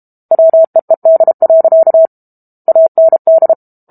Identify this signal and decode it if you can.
Morse code, heard as a keyed tone, about 32 wpm, 650 Hz. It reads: WEED. AND
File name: Book Two / 06-Chapter 6 - 32 WPM